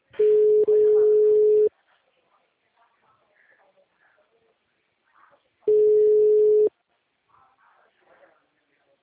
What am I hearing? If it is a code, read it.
Ringback tone (Europe, 425 Hz), with speech.